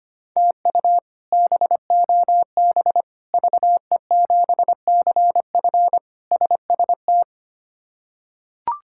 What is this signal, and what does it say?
Morse code, heard as a keyed tone, about 25 wpm, 700 Hz. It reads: TU BOB VE7CF SST
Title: K1USN SST QSO Practice - 25wpm